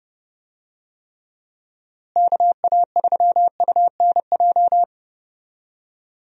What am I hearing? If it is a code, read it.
Morse code, heard as a keyed tone, about 31 wpm, 700 Hz. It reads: KA3UNJ
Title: USA Call Signs - 6 Characters - Repeated 3x - 30wpm